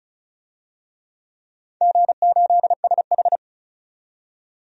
Morse code, heard as a keyed tone, about 34 wpm, 700 Hz. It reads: G8SH
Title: DXCC-100 Call Signs - 4 Characters - 35wpm